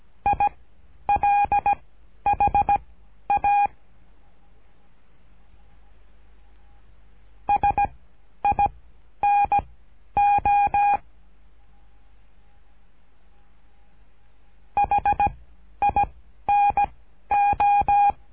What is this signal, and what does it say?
Morse code, heard as a keyed tone, about 17 wpm, spaced as for 11 wpm, 810 Hz. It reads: ILHA SINO HINO
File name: cursos / CURSO DE TELEGRAFIA / Telegrafia - Trilha 13